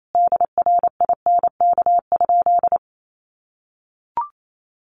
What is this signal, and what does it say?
Morse code, heard as a keyed tone, about 28 wpm, 700 Hz. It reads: DRINK?
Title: Sentences from Top 1500 Words 28wpm